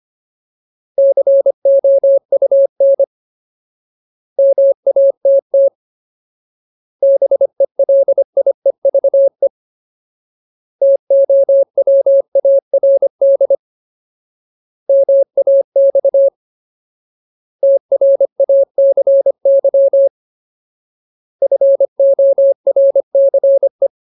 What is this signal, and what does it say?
Morse code, heard as a keyed tone, about 25 wpm, 550 Hz. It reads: COUN MATT BELIEVE TOWARD MAX TRACY FORCE
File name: Ultimate Word List / 25WPM Ultimate